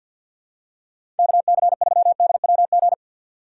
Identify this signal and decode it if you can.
Morse code, heard as a keyed tone, about 52 wpm, 700 Hz. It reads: KC3DWG
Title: USA Call Signs - 6 Characters - 4x Character Spacing - 50wpm